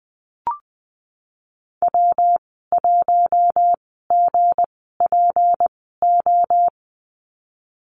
Morse code, heard as a keyed tone, about 19 wpm, spaced as for 14 wpm, 700 Hz. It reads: W1GPO
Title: 179 Lesson 29g - Slash - Call Signs 20wpm